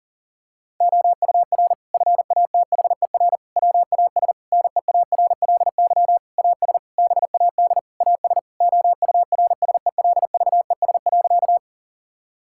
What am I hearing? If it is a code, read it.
Morse code, heard as a keyed tone, about 37 wpm, 700 Hz. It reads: OUR FATHER WAS NEARLY AS BAD AS OURSELVES.
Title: Sentences from Top 1600 Words 40wpm